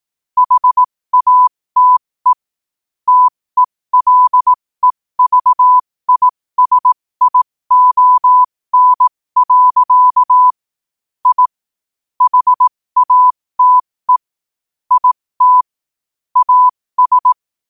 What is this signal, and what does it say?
Morse code, heard as a keyed tone, about 18 wpm, spaced as for 15 wpm, 1000 Hz. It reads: HATE TELEVISION. I HATE IT AS
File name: qotdfiles / 15wpm